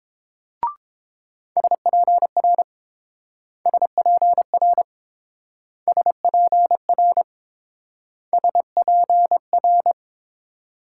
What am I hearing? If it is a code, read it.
Morse code, heard as a keyed tone, about 30 wpm, 700 Hz. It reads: SPR SPR SPR SPR